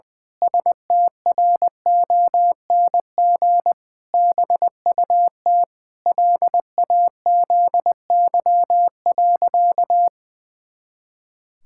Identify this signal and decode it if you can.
Morse code, heard as a keyed tone, about 19 wpm, 700 Hz. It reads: STRONG BUT LAZY.
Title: Pangram Challenge v2 - 20wpm